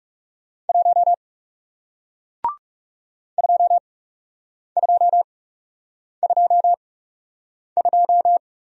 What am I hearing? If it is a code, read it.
Morse code, heard as a keyed tone, about 39 wpm, spaced as for 8 wpm, 700 Hz. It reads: 1 2222